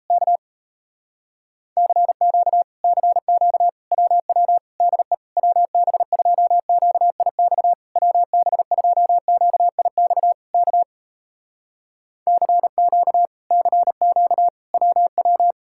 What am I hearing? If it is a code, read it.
Morse code, heard as a keyed tone, about 37 wpm, 700 Hz. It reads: K CQ CQ WW DE WB2QIX WB2QIX K CQ CQ WW